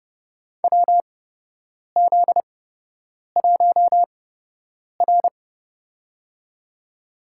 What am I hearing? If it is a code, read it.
Morse code, heard as a keyed tone, about 28 wpm, spaced as for 8 wpm, 700 Hz. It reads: WZ1R
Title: USA Call Signs - 4 Characters - 8x Character Spacing - 30wpm